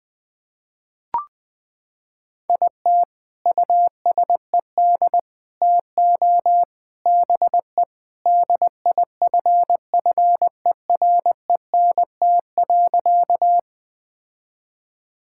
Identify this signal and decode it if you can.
Morse code, heard as a keyed tone, about 20 wpm, 700 Hz. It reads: IT USED TO BE DIFFERENT.
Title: Sentences with 4-Word Phrases - 20wpm